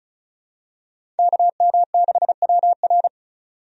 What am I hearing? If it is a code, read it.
Morse code, heard as a keyed tone, about 34 wpm, 700 Hz. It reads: KM6WR
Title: USA Call Signs - 5 Characters - 35wpm